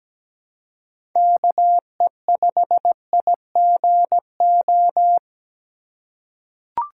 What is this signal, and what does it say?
Morse code, heard as a keyed tone, about 17 wpm, 700 Hz. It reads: KE5IGO
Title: USA Call Signs - 6 Characters - 4x Character Spacing - 17wpm